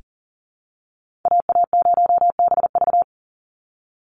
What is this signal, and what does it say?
Morse code, heard as a keyed tone, about 40 wpm, 700 Hz. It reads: AA0BV
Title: NA SOTA Activator Callsigns 40wpm